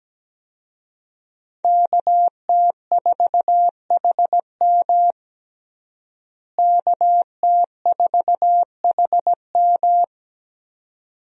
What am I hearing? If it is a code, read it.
Morse code, heard as a keyed tone, about 17 wpm, 700 Hz. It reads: KT4HM KT4HM
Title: USA Call Signs - 5 Characters - Repeated 2x - 17wpm